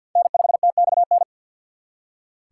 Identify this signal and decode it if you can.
Morse code, heard as a keyed tone, about 55 wpm, 700 Hz. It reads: N5TXN
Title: Speed Racing - USA Call Signs 50wpm